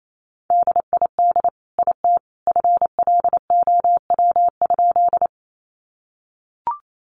Morse code, heard as a keyed tone, about 28 wpm, 700 Hz. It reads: DID IT FLOW?
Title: Sentences from Top 1200 Words 28wpm